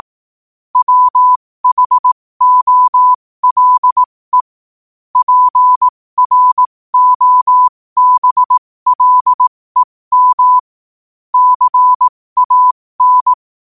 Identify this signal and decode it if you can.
Morse code, heard as a keyed tone, about 18 wpm, spaced as for 15 wpm, 1000 Hz. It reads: WHOLE PROBLEM CAN
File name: qotdfiles / 15wpm